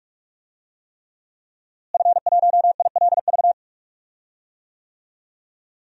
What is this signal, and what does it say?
Morse code, heard as a keyed tone, about 46 wpm, 700 Hz. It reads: F1ILV
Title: DXCC-100 Call Signs - 5 Characters - Repeated 3x - 45wpm